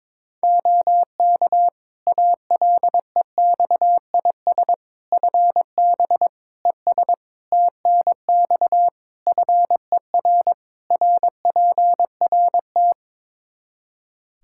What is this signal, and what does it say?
Morse code, heard as a keyed tone, about 21 wpm, 700 Hz. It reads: OK ALEXIS FB ES TNX FER RPRT
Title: Practice QSO - Exchange 2 - Info - 22wpm